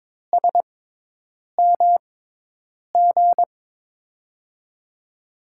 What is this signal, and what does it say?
Morse code, heard as a keyed tone, about 22 wpm, spaced as for 8 wpm, 700 Hz. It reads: SMG
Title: USA Call Signs - 5 Characters - 6x Character Spacing - 22wpm